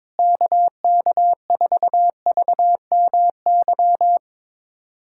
Morse code, heard as a keyed tone, about 21 wpm, 700 Hz. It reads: KK4VMY
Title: USA Callsign DE Callsign 22wpm